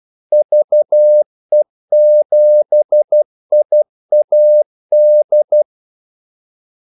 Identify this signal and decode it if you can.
Morse code, heard as a keyed tone, about 12 wpm, 600 Hz. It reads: VE7IAD
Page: 2 ekstra ordmellemrum mellem kaldesignalerne